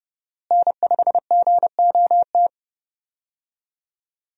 Morse code, heard as a keyed tone, about 31 wpm, 700 Hz. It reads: N5GOT
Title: USA Call Signs - 5 Characters - Repeated 2x - 30wpm